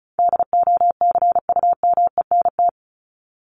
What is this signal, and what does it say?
Morse code, heard as a keyed tone, about 34 wpm, 700 Hz. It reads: DOCUMENT
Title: Top 1751-2000 Words 35wpm